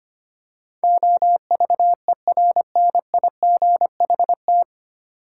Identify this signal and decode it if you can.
Morse code, heard as a keyed tone, about 25 wpm, 700 Hz. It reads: OVERNIGHT
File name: practice / 25wpm / 9 Letter Words - First 3 Letters Given - 25wpm